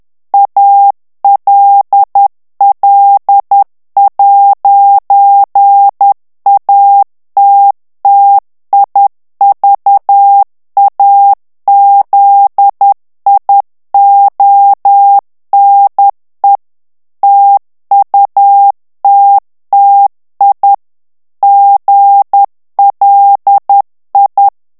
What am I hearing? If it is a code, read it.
Morse code, heard as a keyed tone, about 10 wpm, 800 Hz. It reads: ALL'ATTIVAZIONE TUTTI GLI
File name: morse mp3 / ITALIEN texte clair 1 vitesse 10 mots minute